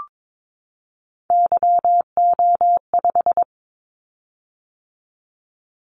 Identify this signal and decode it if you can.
Morse code, heard as a keyed tone, about 22 wpm, 700 Hz. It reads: YO5
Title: Callsign Prefixes for Top 25 DXCC - 22wpm